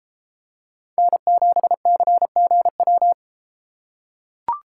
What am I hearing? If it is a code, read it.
Morse code, heard as a keyed tone, about 32 wpm, 700 Hz. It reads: N7CGW